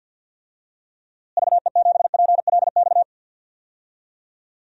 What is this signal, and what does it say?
Morse code, heard as a keyed tone, about 52 wpm, 700 Hz. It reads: VE7PLX
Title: DXCC-100 Call Signs - 6 Characters - Repeated 2x - 50wpm